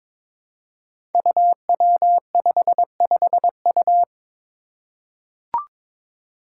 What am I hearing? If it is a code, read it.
Morse code, heard as a keyed tone, about 22 wpm, 700 Hz. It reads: UW55U
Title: Five Challenging Letter-Number Combinations 22wpm